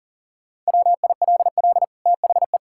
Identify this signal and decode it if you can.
Morse code, heard as a keyed tone, about 40 wpm, 700 Hz. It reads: WILL THE
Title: Sentences from Top 1000 Words - Review Part 6 - 40wpm